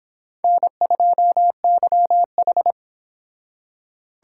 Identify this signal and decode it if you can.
Morse code, heard as a keyed tone, about 32 wpm, 700 Hz. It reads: N2YH